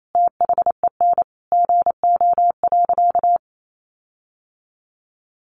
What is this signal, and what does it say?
Morse code, heard as a keyed tone, about 28 wpm, 700 Hz. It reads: THEN GO.
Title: Sentences from Top 1300 Words 28wpm